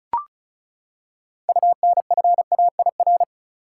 Morse code, heard as a keyed tone, about 36 wpm, 700 Hz. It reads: UNFAIR
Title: Six Letter Words - 35wpm